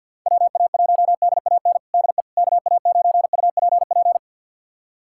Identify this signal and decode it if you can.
Morse code, heard as a keyed tone, about 51 wpm, 700 Hz. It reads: WA1DAN DE KA9UPP
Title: USA Callsign DE Callsign 50wpm